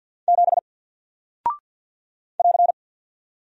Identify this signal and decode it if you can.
Morse code, heard as a keyed tone, about 48 wpm, 700 Hz. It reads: 7 <AR>
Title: Single Character (Letter:Number:Punctuation:Prosign) Alt 50wpm